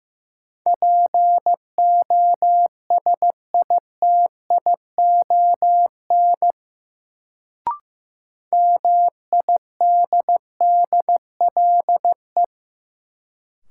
Morse code, heard as a keyed tone, about 15 wpm, 700 Hz. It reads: POSITION MIDDLE
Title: Sentences from Top 900 Words - New Word Drill - 15wpm